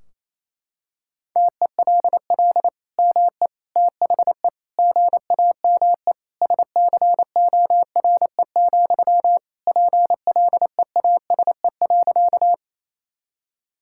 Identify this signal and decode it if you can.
Morse code, heard as a keyed tone, about 28 wpm, 700 Hz. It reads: TELL ME THE GAME SCORE, PLEASE.
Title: Sentences from Top 1600 Words 28wpm